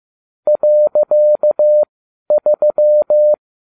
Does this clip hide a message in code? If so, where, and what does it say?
Yes: Morse code from 0.5 to 3.3 s, heard as a keyed tone, about 15 wpm, spaced as for 11 wpm, 600 Hz. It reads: .3